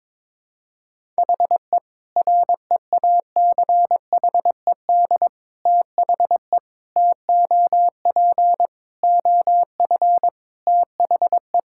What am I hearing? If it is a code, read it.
Morse code, heard as a keyed tone, about 22 wpm, 700 Hz. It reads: HE REACHED THE TOP OF THE
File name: practice / 22wpm / Sentences with 3-Word Phrases - 22wpm